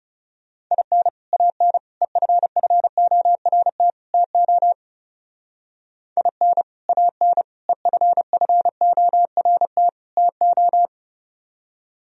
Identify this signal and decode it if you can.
Morse code, heard as a keyed tone, about 36 wpm, 700 Hz. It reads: IN AN EFFORT TO IN AN EFFORT TO
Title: Speed-Racing - Sets of 4 Words 30wpm